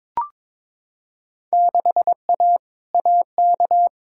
Morse code, heard as a keyed tone, about 22 wpm, 700 Hz. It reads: 6A AK
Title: Field Day Exchanges Encore - 22wpm